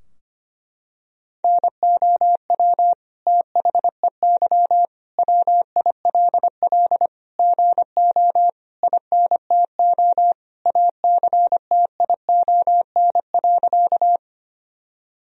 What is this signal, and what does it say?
Morse code, heard as a keyed tone, about 25 wpm, 700 Hz. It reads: NOW THEY WILL GO INTO ACTION.
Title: Sentences from Top 1000 Words - Review Part 4 - 25wpm